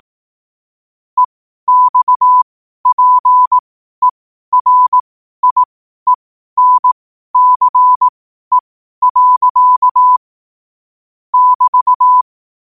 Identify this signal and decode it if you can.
Morse code, heard as a keyed tone, about 18 wpm, spaced as for 12 wpm, 1000 Hz. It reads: EXPERIENCE. <BT>
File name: qotdfiles / 12wpm